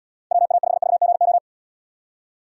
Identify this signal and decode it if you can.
Morse code, heard as a keyed tone, about 74 wpm, 700 Hz. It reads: WA5VGO